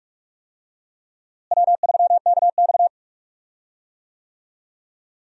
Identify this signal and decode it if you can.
Morse code, heard as a keyed tone, about 42 wpm, 700 Hz. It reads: W3KX
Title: USA Call Signs - 4 Characters - Repeated 2x - 45wpm